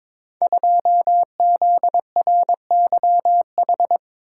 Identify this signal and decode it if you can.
Morse code, heard as a keyed tone, about 22 wpm, 700 Hz. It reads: 2ZRYH
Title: Code Groups - 5 Letter:Numbers - 22wpm